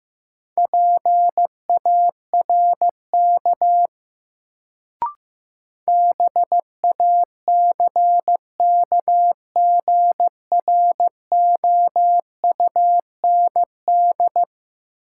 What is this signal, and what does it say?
Morse code, heard as a keyed tone, about 15 wpm, 700 Hz. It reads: PARK BACKGROUND
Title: Sentences from Top 1300 Words - New Word Drill - 15wpm